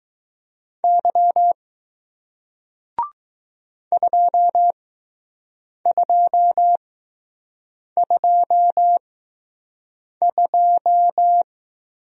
Morse code, heard as a keyed tone, about 20 wpm, spaced as for 6 wpm, 700 Hz. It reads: Y 2222